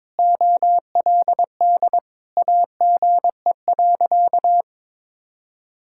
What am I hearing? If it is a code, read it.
Morse code, heard as a keyed tone, about 22 wpm, 700 Hz. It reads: OLD AGE.
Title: Sentences from Top 1000 Words - Review Part 6 - 22wpm